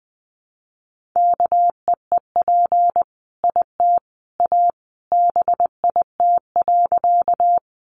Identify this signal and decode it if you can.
Morse code, heard as a keyed tone, about 20 wpm, 700 Hz. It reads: KEEP IT A BIT.
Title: Sentences from Top 900 Words 20wpm